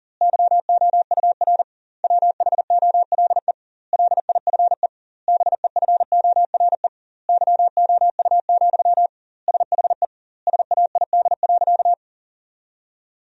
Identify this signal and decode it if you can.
Morse code, heard as a keyed tone, about 37 wpm, 700 Hz. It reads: YOUR WHOLE LIFE BEFORE YOU, SHE SAID.
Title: 2x Word-Spacing - Sentences from Top 400 Words - 40wpm